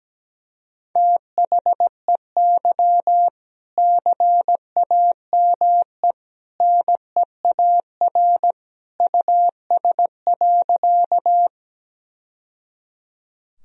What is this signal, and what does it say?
Morse code, heard as a keyed tone, about 17 wpm, 700 Hz. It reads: THEY CAME NEAR US.
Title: Sentences from Top 1000 Words - Review Part 1 - 17wpm